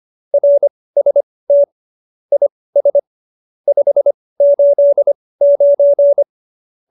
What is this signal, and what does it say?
Morse code, heard as a keyed tone, about 25 wpm, spaced as for 18 wpm, 570 Hz. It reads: RST IS 589
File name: qso208b 18